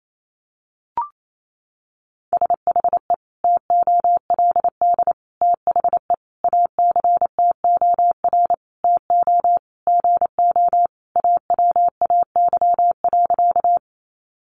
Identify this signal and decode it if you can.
Morse code, heard as a keyed tone, about 28 wpm, 700 Hz. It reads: SHE TOLD THE ACTOR TO GO AWAY.
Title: Sentences from Top 1500 Words 28wpm